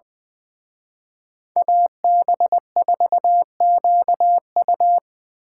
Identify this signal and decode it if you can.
Morse code, heard as a keyed tone, about 23 wpm, 700 Hz. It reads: AB4QU
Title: Speed Racing - USA 2x2 Call Signs 20wpm